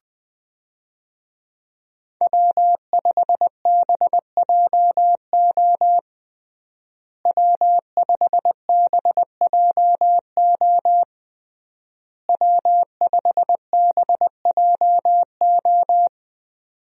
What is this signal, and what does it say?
Morse code, heard as a keyed tone, about 20 wpm, 700 Hz. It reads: W5BJO W5BJO W5BJO
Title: USA Call Signs - 5 Characters - Repeated 3x - 20wpm